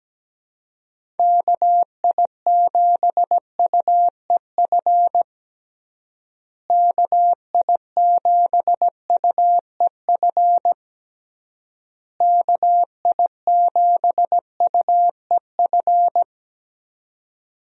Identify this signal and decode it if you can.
Morse code, heard as a keyed tone, about 17 wpm, 700 Hz. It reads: KI7UEF KI7UEF KI7UEF
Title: USA Call Signs - 6 Characters - Repeated 3x - 17wpm